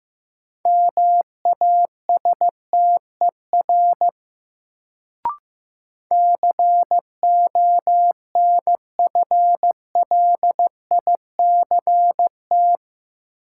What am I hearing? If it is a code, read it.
Morse code, heard as a keyed tone, about 15 wpm, 700 Hz. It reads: MASTER CONFLICT
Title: Sentences from Top 1300 Words - New Word Drill - 15wpm